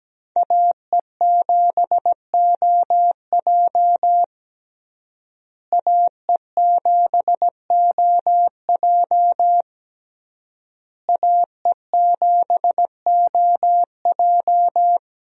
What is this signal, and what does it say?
Morse code, heard as a keyed tone, about 17 wpm, 700 Hz. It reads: AE7OJ AE7OJ AE7OJ
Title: USA Call Signs - 5 Characters - Repeated 3x - 17wpm